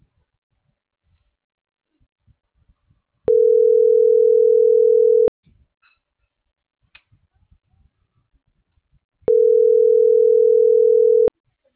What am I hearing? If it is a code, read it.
Ringback tone (North America), 440 + 480 Hz.